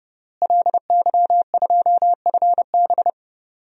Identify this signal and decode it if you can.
Morse code, heard as a keyed tone, about 30 wpm, 700 Hz. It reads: LY2FB
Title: DXCC-100 Call Signs - 5 Characters - Repeated 2x - 30wpm